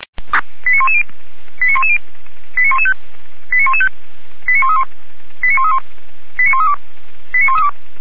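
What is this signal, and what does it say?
Five-tone selective call (ZVEI), heard as quick runs of five tones, tones of about 70 ms. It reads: 89110 89110 89115 89115 89131 89131 89132 89132
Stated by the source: ZVEI five-tone selective call 89109 received by SDR.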